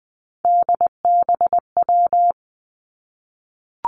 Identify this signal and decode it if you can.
Morse code, heard as a keyed tone, about 20 wpm, 700 Hz. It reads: DBW
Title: Three Challenging Letter-Number Combinations 20wpm